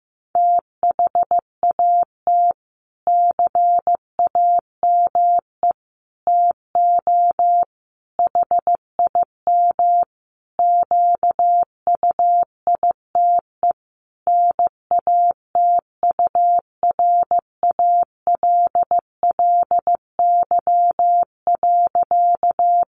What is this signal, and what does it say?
Morse code, heard as a keyed tone, about 15 wpm, 700 Hz. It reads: THAT CAME TO HIM QUITE NATURALLY.
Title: Sentences from Top 1300 Words 15wpm